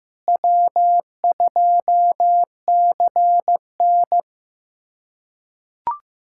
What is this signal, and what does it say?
Morse code, heard as a keyed tone, about 15 wpm, 700 Hz. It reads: W2CN
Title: USA Callsign DE Callsign 15wpm